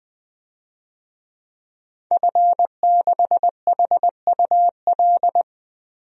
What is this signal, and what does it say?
Morse code, heard as a keyed tone, about 21 wpm, 700 Hz. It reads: F6HUL
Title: DXCC-100 Call Signs - 5 Characters - Repeated 2x - 20wpm